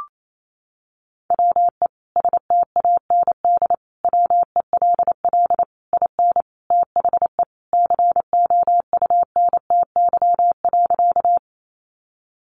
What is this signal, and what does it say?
Morse code, heard as a keyed tone, about 28 wpm, 700 Hz. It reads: WE STAND WELL IN THE COUNTY.
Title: Sentences from Top 1200 Words 28wpm